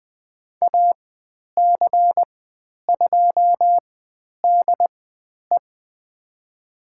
Morse code, heard as a keyed tone, about 20 wpm, spaced as for 10 wpm, 700 Hz. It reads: AC2DE
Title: 095 Lesson 16g - Letter c - Call Signs 20wpm